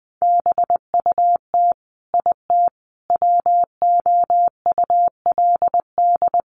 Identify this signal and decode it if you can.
Morse code, heard as a keyed tone, about 19 wpm, 700 Hz. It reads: BUT IT WOULD
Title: Sentences from Top 1500 Words 20wpm